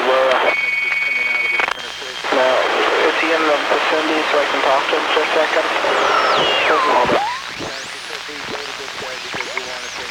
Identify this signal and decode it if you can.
Dial-up modem handshake, with speech.